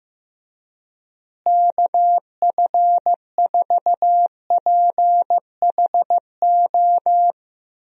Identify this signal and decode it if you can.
Morse code, heard as a keyed tone, about 16 wpm, 700 Hz. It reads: KF4PHO
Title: USA Call Signs - 6 Characters - Repeated 3x - 15wpm